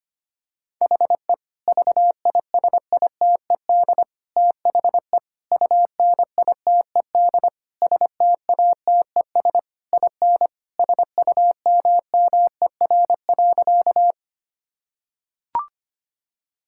Morse code, heard as a keyed tone, about 25 wpm, 700 Hz. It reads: HE VISITED THE UNITED STATES IN SUMMER.
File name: practice / 25wpm / Sentences with 4-Word Phrases - 25wpm